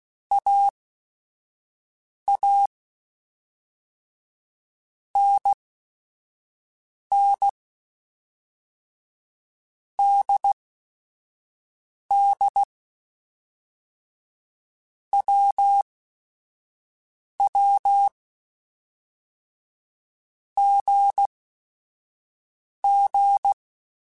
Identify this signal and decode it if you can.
Morse code, heard as a keyed tone, about 16 wpm, spaced as for 5 wpm, 800 Hz. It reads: AA NN DD WW GG